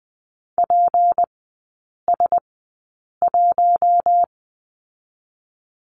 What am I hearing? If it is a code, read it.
Morse code, heard as a keyed tone, about 20 wpm, spaced as for 8 wpm, 700 Hz. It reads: PS1
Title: 152 Lesson 25e - Letter p - Three Characters 20wpm